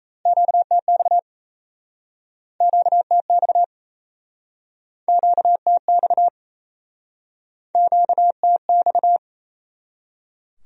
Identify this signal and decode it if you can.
Morse code, heard as a keyed tone, about 37 wpm, 700 Hz. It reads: QTX QTX QTX QTX